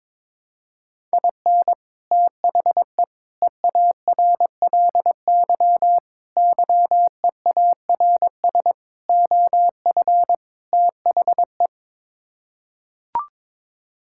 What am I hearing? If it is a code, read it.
Morse code, heard as a keyed tone, about 22 wpm, 700 Hz. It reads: IN THE EARLY YEARS OF THE
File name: practice / 22wpm / Sets of 6 Words - 22wpm